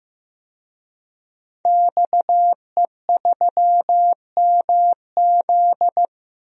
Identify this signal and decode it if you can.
Morse code, heard as a keyed tone, about 15 wpm, 700 Hz. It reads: XE3MZ
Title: DXCC-100 Call Signs - 5 Characters - Repeated 2x - 15wpm